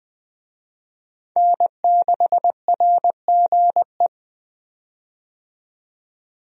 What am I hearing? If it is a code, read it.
Morse code, heard as a keyed tone, about 21 wpm, 700 Hz. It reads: N6RGE
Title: USA Call Signs - 5 Characters - 20wpm